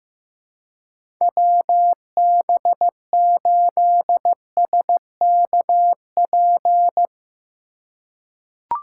Morse code, heard as a keyed tone, about 15 wpm, 700 Hz. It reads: WB8SKP
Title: K1USN SST QSO Practice - 15wpm